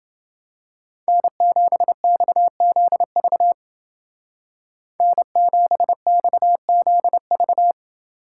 Morse code, heard as a keyed tone, about 29 wpm, 700 Hz. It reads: N7XZV N7XZV